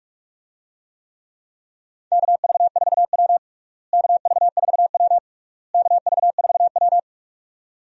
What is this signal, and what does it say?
Morse code, heard as a keyed tone, about 46 wpm, 700 Hz. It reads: KV4W KV4W KV4W
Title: USA Call Signs - 4 Characters - Repeated 3x - 45wpm